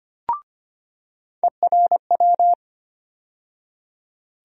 Morse code, heard as a keyed tone, about 25 wpm, 700 Hz. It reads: ERW
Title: USA Call Sign Postfixes - 3 Letters 25wpm